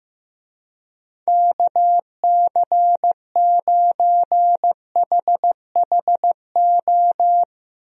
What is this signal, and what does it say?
Morse code, heard as a keyed tone, about 15 wpm, 700 Hz. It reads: KC9HHO
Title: USA Call Signs - 6 Characters - Repeated 2x - 15wpm